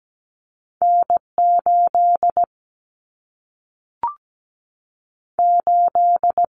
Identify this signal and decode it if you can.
Morse code, heard as a keyed tone, about 17 wpm, 700 Hz. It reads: N8 8
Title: Common Cut Numbers - 2-Digit 17wpm